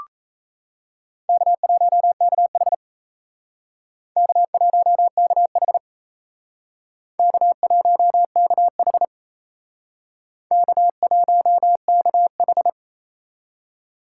Morse code, heard as a keyed tone, about 35 wpm, 700 Hz. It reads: K1KH K1KH K1KH K1KH
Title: Speed Racing - USA 1x2 Call Signs 28wpm